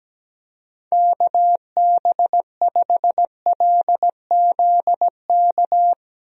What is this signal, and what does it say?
Morse code, heard as a keyed tone, about 17 wpm, 700 Hz. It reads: KB5LZK
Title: USA Call Signs - 6 Characters - Repeated 2x - 17wpm